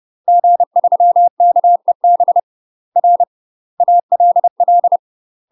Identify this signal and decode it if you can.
Morse code, heard as a keyed tone, about 30 wpm, 700 Hz. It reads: G3KEB R ALL
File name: UK MASTER QSOs TRAINING/30WPM